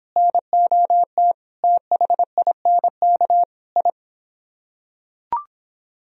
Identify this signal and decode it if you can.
Morse code, heard as a keyed tone, about 26 wpm, 700 Hz. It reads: NOT THINK I